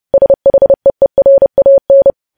Morse code, heard as a keyed tone, about 30 wpm, 550 Hz. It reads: SHEERAN
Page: News Headlines in Morse Code at 30 WPM